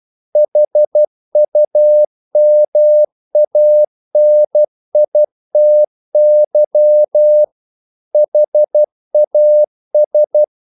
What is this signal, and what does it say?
Morse code, heard as a keyed tone, about 12 wpm, 600 Hz. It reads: HUMANITY HAS
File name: CW / Practice / EnglishQuotations0112